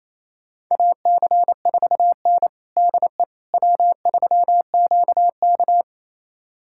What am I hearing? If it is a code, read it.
Morse code, heard as a keyed tone, about 28 wpm, 700 Hz. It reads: AC4N DE W3QK
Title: USA Callsign DE Callsign - 4 Characters - 28wpm